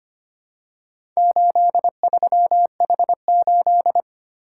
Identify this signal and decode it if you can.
Morse code, heard as a keyed tone, about 25 wpm, 700 Hz. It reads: 83H8
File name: practice / 25wpm / Four Letter-Number Combinations - 25wpm